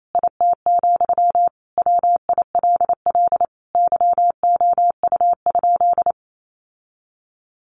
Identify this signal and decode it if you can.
Morse code, heard as a keyed tone, about 28 wpm, 700 Hz. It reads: IT, WILL YOU?
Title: Sentences from Top 1100 Words 28wpm